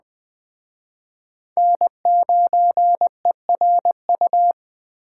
Morse code, heard as a keyed tone, about 21 wpm, 700 Hz. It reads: N9ERU
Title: Speed Racing - USA 1x3 Call Signs 17wpm